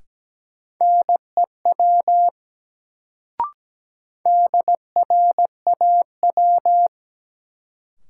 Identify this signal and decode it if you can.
Morse code, heard as a keyed tone, about 17 wpm, 700 Hz. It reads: NEW DRAW
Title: Sentences from Top 300 Words - New Word Drill - 17wpm